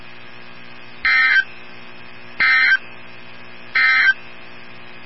Emergency Alert System (SAME) tones, heard as harsh buzzing bursts. It is heard from 0.9 to 4.2 s.
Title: NOAA Weather Radio 162.550